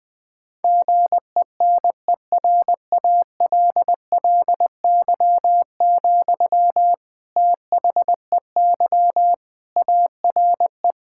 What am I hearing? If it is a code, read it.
Morse code, heard as a keyed tone, about 20 wpm, 700 Hz. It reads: GENERALLY, THEY ARE
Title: Sentences from Top 1000 Words - Review Part 6 - 20wpm